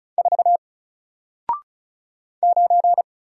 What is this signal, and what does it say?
Morse code, heard as a keyed tone, about 36 wpm, 700 Hz. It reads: 4 9
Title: Single Character (Letter:Number:Punctuation:Prosign) Alt 35wpm